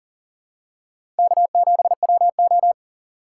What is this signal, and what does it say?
Morse code, heard as a keyed tone, about 38 wpm, 700 Hz. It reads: K7WO
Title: USA Call Signs - 4 Characters - 3x Character Spacing - 40wpm